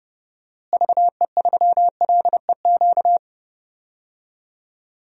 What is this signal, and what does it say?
Morse code, heard as a keyed tone, about 32 wpm, 700 Hz. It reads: VE3LEQ
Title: DXCC-100 Call Signs - 6 Characters - Repeated 3x - 30wpm